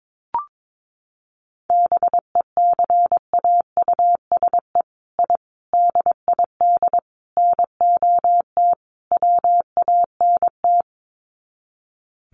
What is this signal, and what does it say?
Morse code, heard as a keyed tone, about 22 wpm, 700 Hz. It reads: BECAUSE I DID NOT WANT
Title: Sets of 5 Words Encore2 22wpm